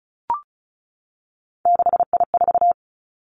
Morse code, heard as a keyed tone, about 36 wpm, 700 Hz. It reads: 6I4
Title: Three Challenging Letter-Number Combinations 35wpm